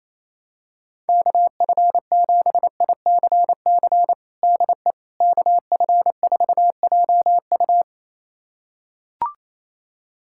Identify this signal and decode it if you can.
Morse code, heard as a keyed tone, about 28 wpm, 700 Hz. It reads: KF7ICC DE KF4JU
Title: USA Callsign DE Callsign 28wpm